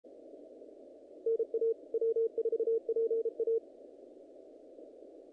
Morse code, heard as a keyed tone, about 33 wpm, 450 Hz, with radio noise. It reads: NA W4PA
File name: contests / naqp-cw-k5tr-op-k5pi- -2100